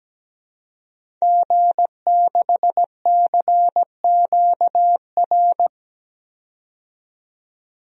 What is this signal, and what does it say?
Morse code, heard as a keyed tone, about 17 wpm, 700 Hz. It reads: G6CQR
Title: DXCC-100 Call Signs - 5 Characters - 17wpm